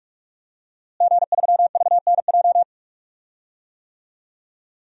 Morse code, heard as a keyed tone, about 46 wpm, 700 Hz. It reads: G3VNJ
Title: DXCC-100 Call Signs - 5 Characters - Repeated 3x - 45wpm